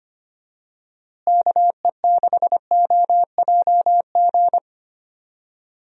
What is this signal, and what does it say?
Morse code, heard as a keyed tone, about 25 wpm, 700 Hz. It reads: KE6OJG
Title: USA Call Signs - 6 Characters - Repeated 3x - 25wpm